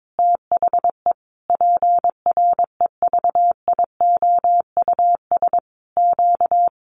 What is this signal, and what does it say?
Morse code, heard as a keyed tone, about 22 wpm, 700 Hz. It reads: THE PREVIOUS Q
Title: Sentences from Top 1500 Words 22wpm